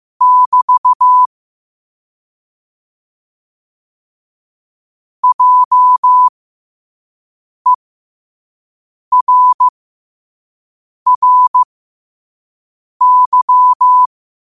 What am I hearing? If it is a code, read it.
Morse code, heard as a keyed tone, about 15 wpm, spaced as for 5 wpm, 1000 Hz. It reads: <BT> JERRY
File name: qotdfiles / 5wpm